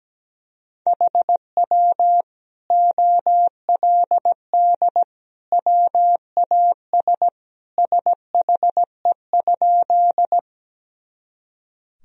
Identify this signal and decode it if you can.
Morse code, heard as a keyed tone, about 17 wpm, 700 Hz. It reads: HW OLD WAS SHE?
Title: Sentences from Top 200 Words and Abbreviations 17wpm